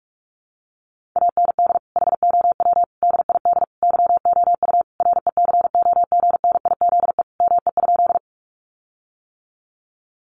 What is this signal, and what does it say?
Morse code, heard as a keyed tone, about 44 wpm, 700 Hz. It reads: AND HOW DID YOU RECOGNIZE ME?
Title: Sentences from Top 1400 Words 45wpm